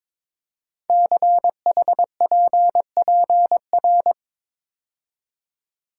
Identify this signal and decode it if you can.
Morse code, heard as a keyed tone, about 22 wpm, 700 Hz. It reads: CHPPR
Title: Code Groups - 5 Letters - 22wpm